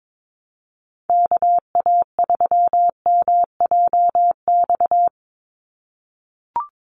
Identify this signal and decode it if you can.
Morse code, heard as a keyed tone, about 22 wpm, 700 Hz. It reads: KA3MJX
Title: USA Call Signs 22wpm